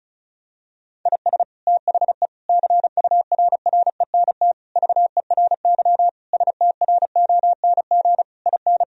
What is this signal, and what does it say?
Morse code, heard as a keyed tone, about 34 wpm, 700 Hz. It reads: IS THE CURRENT VERY STRONG IN
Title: Sentences from Top 1000 Words - Review Part 4 - 35wpm